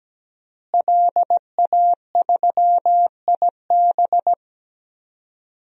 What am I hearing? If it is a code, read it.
Morse code, heard as a keyed tone, about 17 wpm, 700 Hz. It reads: LA3IB
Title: DXCC-100 Call Signs - 5 Characters - Repeated 3x - 17wpm